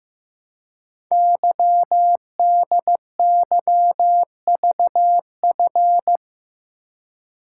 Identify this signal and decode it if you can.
Morse code, heard as a keyed tone, about 15 wpm, 700 Hz. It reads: YDYVF
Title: Code Groups - 5 Letters - 15wpm